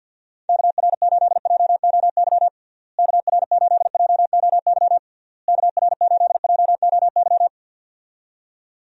Morse code, heard as a keyed tone, about 52 wpm, 700 Hz. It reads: KR8JOY KR8JOY KR8JOY
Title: USA Call Signs - 6 Characters - Repeated 3x - 50wpm